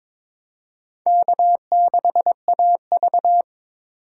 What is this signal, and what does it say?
Morse code, heard as a keyed tone, about 22 wpm, 700 Hz. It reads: K6AV
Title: USA Call Signs - 4 Characters - 3x Character Spacing - 22wpm